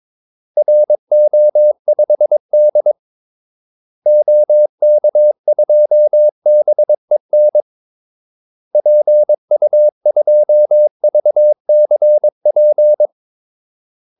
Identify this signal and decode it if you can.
Morse code, heard as a keyed tone, about 22 wpm, 600 Hz. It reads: RO5D OK2BEN PU2VCP
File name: Callsigns-22wpm